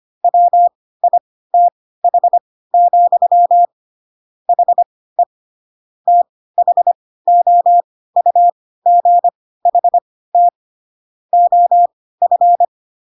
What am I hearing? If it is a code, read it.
Morse code, heard as a keyed tone, about 25 wpm, spaced as for 16 wpm, 700 Hz. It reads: WITH, HE THOUGHT OF